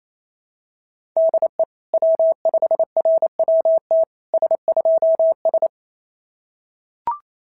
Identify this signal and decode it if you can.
Morse code, heard as a keyed tone, about 29 wpm, 690 Hz. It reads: DE W5RWT S2S
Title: SOTA QSO Practice 28wpm